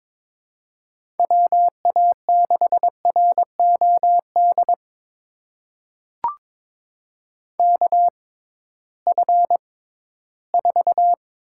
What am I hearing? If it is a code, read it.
Morse code, heard as a keyed tone, about 22 wpm, spaced as for 14 wpm, 700 Hz. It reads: WA6ROD KF4
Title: USA Call Signs - 6 Characters - 6x Character Spacing - 22wpm